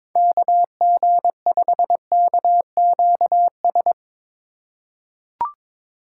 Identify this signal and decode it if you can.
Morse code, heard as a keyed tone, about 22 wpm, 700 Hz. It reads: KG5KQS
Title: USA Call Signs - 4x Character Spacing - 22wpm